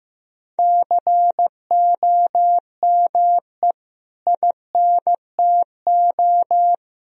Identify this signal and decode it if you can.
Morse code, heard as a keyed tone, about 16 wpm, 700 Hz. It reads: COME INTO
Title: Sentences from Top 1000 Words - Review Part 4 - 15wpm